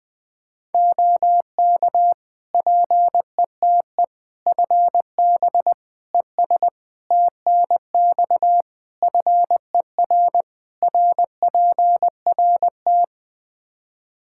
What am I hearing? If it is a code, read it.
Morse code, heard as a keyed tone, about 20 wpm, 700 Hz. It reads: OK PETE FB ES TNX FER RPRT
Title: Practice QSO - Exchange 2 - Info - 20wpm